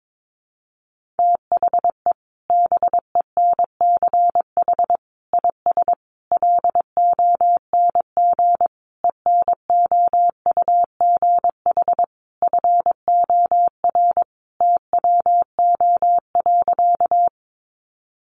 Morse code, heard as a keyed tone, about 22 wpm, 700 Hz. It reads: THE BENCH IS LONG ENOUGH FOR TWO.
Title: Sentences from Top 1400 Words 22wpm